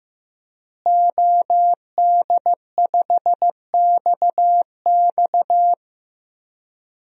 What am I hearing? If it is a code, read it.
Morse code, heard as a keyed tone, about 15 wpm, 700 Hz. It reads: OD5XX
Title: DXCC-100 Call Signs - 5 Characters - Repeated 2x - 15wpm